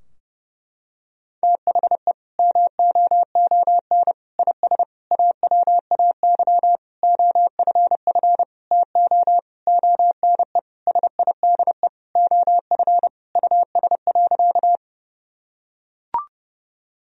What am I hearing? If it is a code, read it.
Morse code, heard as a keyed tone, about 30 wpm, 700 Hz. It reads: THE MOON IS AWAY OFF TO ONE SIDE OF US.
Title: Sentences from Top 1000 Words - Review Part 3 - 30wpm